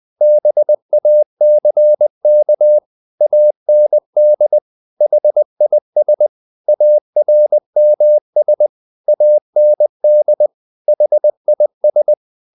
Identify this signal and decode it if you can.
Morse code, heard as a keyed tone, about 20 wpm, 600 Hz. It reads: BACK AND HIS ARMS AND HIS
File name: the old man and the sea / the old man and the sea part 1 20wpm